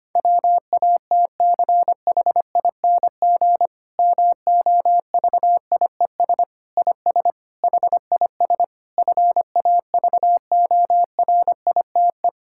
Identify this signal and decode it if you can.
Morse code, heard as a keyed tone, about 25 wpm, 700 Hz. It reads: WATCHING MOVIES IS HIS FAVORITE
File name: practice / 25wpm / Difficult Sentences with Hints - 25wpm